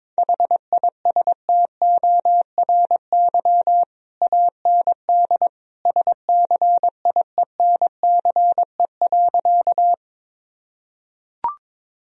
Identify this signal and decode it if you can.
Morse code, heard as a keyed tone, about 22 wpm, 700 Hz. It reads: HISTORY AND SCIENCE.
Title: Sentences from Top 1000 Words - Review Part 2 - 22wpm